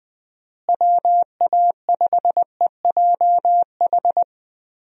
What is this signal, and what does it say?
Morse code, heard as a keyed tone, about 21 wpm, 700 Hz. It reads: WA5EJH
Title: USA Call Signs - 6 Characters - Repeated 2x - 20wpm